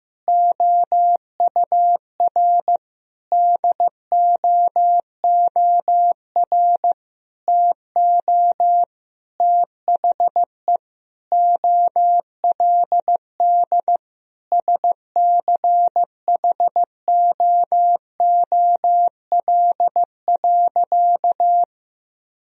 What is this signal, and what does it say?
Morse code, heard as a keyed tone, about 16 wpm, 700 Hz. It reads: OUR DOOR TO THE OLD SCHOOL.
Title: Sentences from Top 1000 Words - Review Part 1 - 15wpm